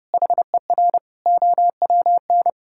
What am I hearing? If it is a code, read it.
Morse code, heard as a keyed tone, about 30 wpm, 700 Hz. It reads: HER OWN
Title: Sentences from Top 1000 Words - Review Part 6 - 30wpm